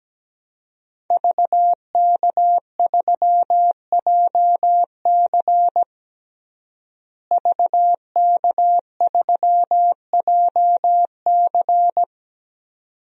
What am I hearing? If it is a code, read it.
Morse code, heard as a keyed tone, about 17 wpm, 700 Hz. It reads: VK3JC VK3JC
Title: DXCC-100 Call Signs - 5 Characters - Repeated 2x - 17wpm